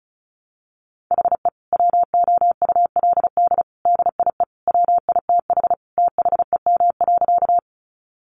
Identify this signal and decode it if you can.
Morse code, heard as a keyed tone, about 34 wpm, 700 Hz. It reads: HE WOULD DIE WITH THEM.
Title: Sentences from Top 1300 Words 35wpm